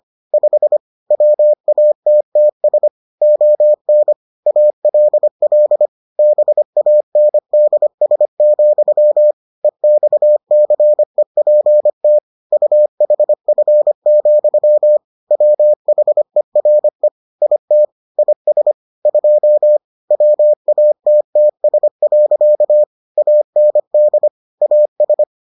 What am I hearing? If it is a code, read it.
Morse code, heard as a keyed tone, about 25 wpm, 600 Hz. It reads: 5 WATTS ON ALL BANDS, EXCEPT UHF, WHERE IT IS 2 WATTS. AND AS